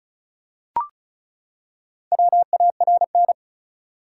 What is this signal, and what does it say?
Morse code, heard as a keyed tone, about 34 wpm, 700 Hz. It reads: WARN
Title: Four Letter Words - 35wpm